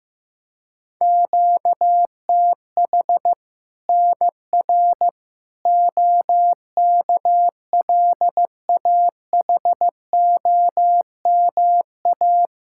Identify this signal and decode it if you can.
Morse code, heard as a keyed tone, about 15 wpm, 700 Hz. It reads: QTH NR OKLAHOMA
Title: Practice QSO - Exchange 1 - Beginning - 15wpm